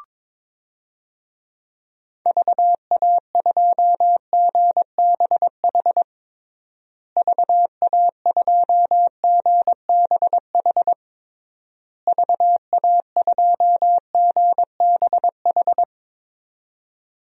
Morse code, heard as a keyed tone, about 22 wpm, 700 Hz. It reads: VA2GBH VA2GBH VA2GBH
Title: DXCC-100 Call Signs - 6 Characters - Repeated 3x - 22wpm